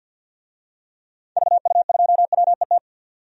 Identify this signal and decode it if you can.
Morse code, heard as a keyed tone, about 48 wpm, 700 Hz. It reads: VU2WET
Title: DXCC-100 Call Signs - 6 Characters - 50wpm